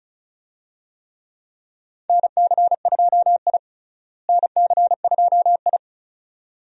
Morse code, heard as a keyed tone, about 36 wpm, 700 Hz. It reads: NC2I NC2I
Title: USA Call Signs - 4 Characters - Repeated 2x - 35wpm